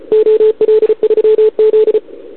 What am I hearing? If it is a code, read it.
Morse code, heard as a keyed tone, about 34 wpm, 440 Hz, with radio noise. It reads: OL3Z
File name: CQWWCW2021 / OL3Z